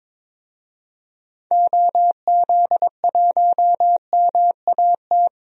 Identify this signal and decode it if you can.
Morse code, heard as a keyed tone, about 22 wpm, 700 Hz. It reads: OZ1MAT
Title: DXCC-100 Call Signs - 6 Characters - 22wpm